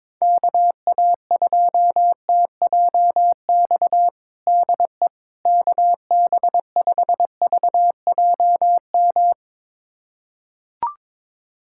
Morse code, heard as a keyed tone, about 22 wpm, 700 Hz. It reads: KA2TJX DE KB5VJM
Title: USA Callsign DE Callsign 22wpm